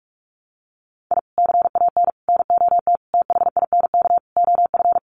Morse code, heard as a keyed tone, about 44 wpm, 700 Hz. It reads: I CAN NOT THINK OF
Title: Sets of 5 Words 45wpm